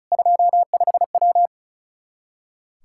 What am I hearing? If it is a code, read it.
Morse code, heard as a keyed tone, about 36 wpm, 700 Hz. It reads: 25W
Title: Practice QSO - Exchange 2 - Info - 35wpm